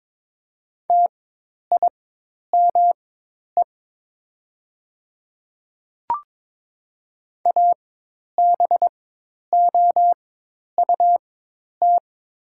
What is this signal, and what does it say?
Morse code, heard as a keyed tone, about 22 wpm, spaced as for 10 wpm, 700 Hz. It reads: TIME ABOUT
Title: 4x Character Spacing - Top 100 Words - 22wpm